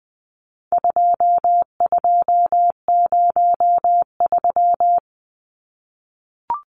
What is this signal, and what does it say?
Morse code, heard as a keyed tone, about 20 wpm, 700 Hz. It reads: 2203
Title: Numbers - 4-Digit 20wpm